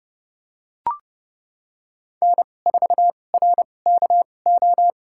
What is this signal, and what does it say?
Morse code, heard as a keyed tone, about 30 wpm, spaced as for 22 wpm, 700 Hz. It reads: N4RKO
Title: USA Call Signs - 5 Characters - 2x Character Spacing - 30wpm